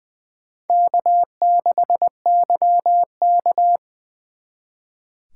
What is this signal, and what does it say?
Morse code, heard as a keyed tone, about 20 wpm, 700 Hz. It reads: K6YK
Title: 2023 SOTA Chaser Call Signs - 20wpm